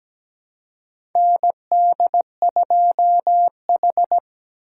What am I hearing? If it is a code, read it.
Morse code, heard as a keyed tone, about 17 wpm, 700 Hz. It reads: ND2H
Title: USA Call Signs - 4 Characters - 2x Character Spacing - 17wpm